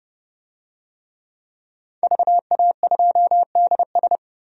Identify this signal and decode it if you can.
Morse code, heard as a keyed tone, about 30 wpm, 700 Hz. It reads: VA2DS
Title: DXCC-100 Call Signs - 5 Characters - 30wpm